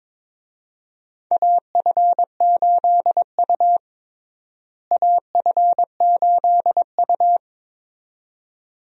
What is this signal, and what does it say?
Morse code, heard as a keyed tone, about 22 wpm, 700 Hz. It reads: AF8U AF8U
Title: USA Call Signs - 4 Characters - Repeated 2x - 22wpm